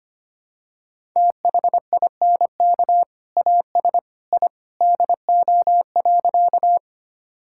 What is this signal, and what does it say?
Morse code, heard as a keyed tone, about 25 wpm, 700 Hz. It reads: THINK AS I DO.
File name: practice / 25wpm / Sentences with 2-Word Phrases - 25wpm